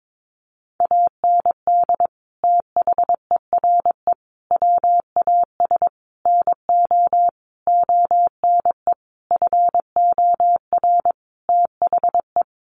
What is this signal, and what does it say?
Morse code, heard as a keyed tone, about 22 wpm, 700 Hz. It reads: AND THERE WAS NO ONE FOR THE
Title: Sentences from Top 1500 Words 22wpm